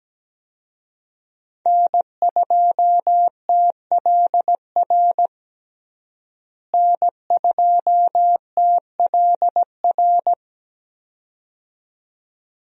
Morse code, heard as a keyed tone, about 17 wpm, 700 Hz. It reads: N2TLR N2TLR
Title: USA Call Signs - 5 Characters - Repeated 2x - 17wpm